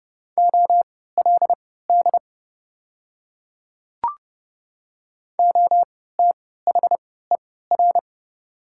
Morse code, heard as a keyed tone, about 30 wpm, spaced as for 17 wpm, 700 Hz. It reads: OLD OTHER
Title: 3x Character Spacing - Top 200 Words - 30wpm